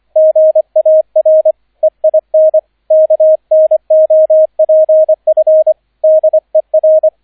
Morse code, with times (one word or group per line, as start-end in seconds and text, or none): 0.2-1.5 GAR
1.8-2.6 EIN
2.9-5.7 KNOPF
6.0-7.1 DER